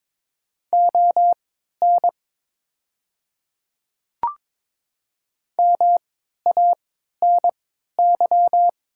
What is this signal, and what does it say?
Morse code, heard as a keyed tone, about 23 wpm, spaced as for 13 wpm, 700 Hz. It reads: ON MANY
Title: 3x Character Spacing - Top 100 Words - 22wpm